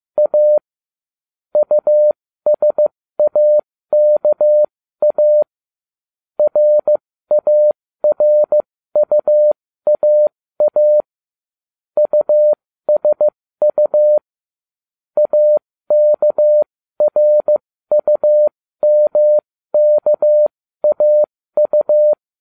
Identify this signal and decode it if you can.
Morse code, heard as a keyed tone, about 15 wpm, spaced as for 13 wpm, 600 Hz. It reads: A USAKA RARUAA USU AKRUMKAU